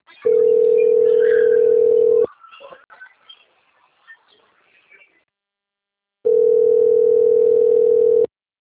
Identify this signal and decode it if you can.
Ringback tone (North America), 440 + 480 Hz.
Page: Campo Grande 1